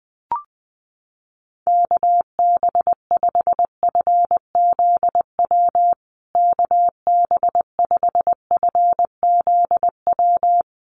Morse code, heard as a keyed tone, about 20 wpm, 700 Hz. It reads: KB5FZW KB5FZW
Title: USA Call Signs Repeated 3x 20wpm